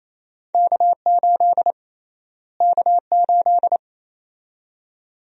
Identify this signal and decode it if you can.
Morse code, heard as a keyed tone, about 28 wpm, 700 Hz. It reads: K8 K8
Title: USA Call Sign Prefixes - Repeated 2x - 28wpm